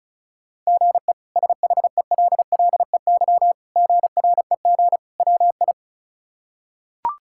Morse code, heard as a keyed tone, about 36 wpm, 700 Hz. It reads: GE SHELLEY GREG WI
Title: K1USN SST QSO Practice - 35wpm